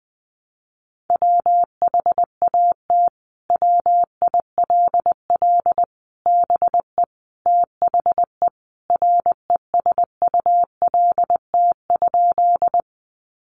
Morse code, heard as a keyed tone, about 20 wpm, 700 Hz. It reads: WHAT WILL BE THE RESULT?
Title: Sentences from Top 700 Words 20wpm